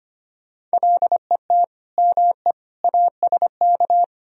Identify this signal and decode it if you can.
Morse code, heard as a keyed tone, about 33 wpm, 700 Hz. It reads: LET ME ASK